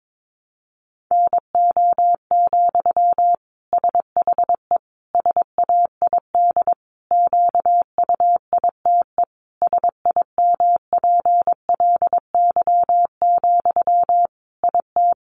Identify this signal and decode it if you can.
Morse code, heard as a keyed tone, about 22 wpm, 700 Hz. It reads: NO, SHE SAID QUITE SIMPLY, IT
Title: Sentences from Top 800 Words 22wpm